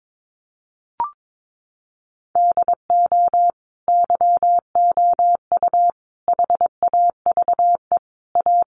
Morse code, heard as a keyed tone, about 22 wpm, 700 Hz. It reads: DO YOU HAVE A
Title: Sentences from Top 1400 Words 22wpm